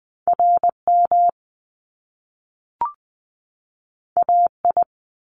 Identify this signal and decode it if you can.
Morse code, heard as a keyed tone, about 20 wpm, 700 Hz. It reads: RM AI
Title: Top Two Letter Combinations in Words 20wpm